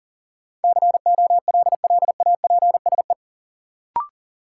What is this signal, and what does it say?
Morse code, heard as a keyed tone, about 40 wpm, 700 Hz. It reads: COLLAPSE
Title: Sets of 5 Words Encore - New Word Drill - 40wpm